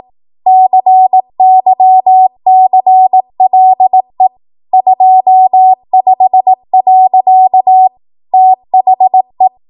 Morse code, heard as a keyed tone, about 18 wpm, 750 Hz. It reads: CYCLE 25. THE
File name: Archive / 18 WPM / 18WPM